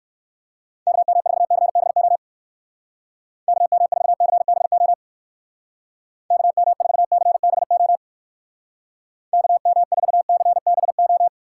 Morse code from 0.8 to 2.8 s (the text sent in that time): KM4CBO